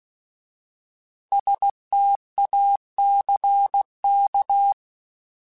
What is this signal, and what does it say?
Morse code, heard as a keyed tone, about 16 wpm, 800 Hz. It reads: STACK